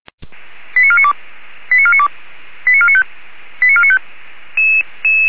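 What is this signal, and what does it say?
Five-tone selective call (ZVEI), heard as quick runs of five tones, tones of about 80 ms. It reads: 89492 89492 89495 89495
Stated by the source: ZVEI five-tone call received by SDR, from Feuerwehr Orlamünde.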